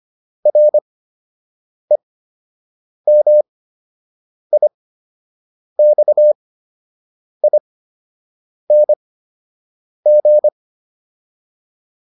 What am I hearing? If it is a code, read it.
Morse code, heard as a keyed tone, about 25 wpm, spaced as for 7 wpm, 600 Hz. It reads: REMIXING
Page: fundamental-practice-files